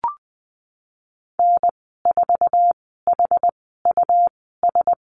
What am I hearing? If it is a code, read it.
Morse code, heard as a keyed tone, about 20 wpm, spaced as for 14 wpm, 700 Hz. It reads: N4HUS
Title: 122 Lesson 20g - Number 3 - Call Signs 20wpm